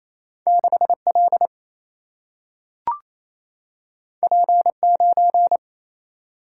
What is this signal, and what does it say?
Morse code, heard as a keyed tone, about 28 wpm, 700 Hz. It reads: 6 L P 9
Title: Single-Letter Single-Number 28wpm